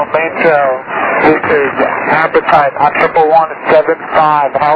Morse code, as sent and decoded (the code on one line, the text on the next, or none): none